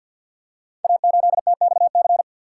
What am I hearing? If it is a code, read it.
Morse code, heard as a keyed tone, about 51 wpm, 690 Hz. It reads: A8TXC
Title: Code Groups - 5 Letter:Numbers - 50wpm